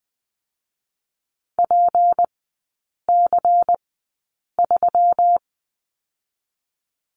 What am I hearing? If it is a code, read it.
Morse code, heard as a keyed tone, about 20 wpm, 700 Hz. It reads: P C 3
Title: 198 Lesson 32e - Letter k - Three Characters 20wpm